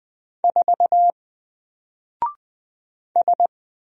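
Morse code, heard as a keyed tone, about 20 wpm, 700 Hz. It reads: 4 S
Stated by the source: Single letter and number Morse drill at 20 wpm, from N4LCD.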